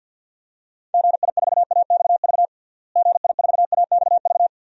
Morse code, heard as a keyed tone, about 46 wpm, 700 Hz. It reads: GI4AXV GI4AXV
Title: DXCC-100 Call Signs - 6 Characters - Repeated 3x - 50wpm